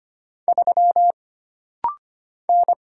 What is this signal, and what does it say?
Morse code, heard as a keyed tone, about 25 wpm, 700 Hz. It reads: 3 N
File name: Single Character (Letter:Number:Punctuation:Prosign) - Rapid-Fire - 25wpm